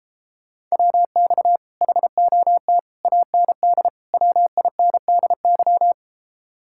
Morse code, heard as a keyed tone, about 33 wpm, 700 Hz. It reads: WX HOT AND WINDY